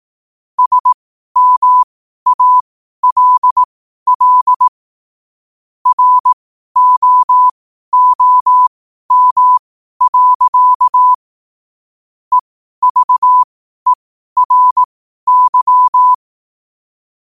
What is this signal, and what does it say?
Morse code, heard as a keyed tone, about 18 wpm, spaced as for 12 wpm, 1000 Hz. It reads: SMALL ROOM. EVERY
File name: qotdfiles / 12wpm